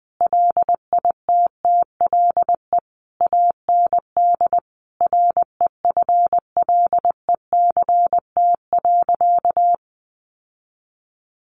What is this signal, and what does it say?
Morse code, heard as a keyed tone, about 20 wpm, 700 Hz. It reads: LITTLE AND REFLECT.
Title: Sentences from Top 1500 Words 20wpm